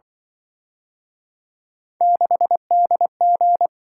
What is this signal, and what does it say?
Morse code, heard as a keyed tone, about 25 wpm, 700 Hz. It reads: 6DG